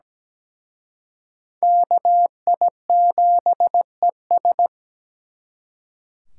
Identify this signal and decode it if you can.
Morse code, heard as a keyed tone, about 20 wpm, 700 Hz. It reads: KI7ES
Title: Speed Racing - USA 2x2 Call Signs 17wpm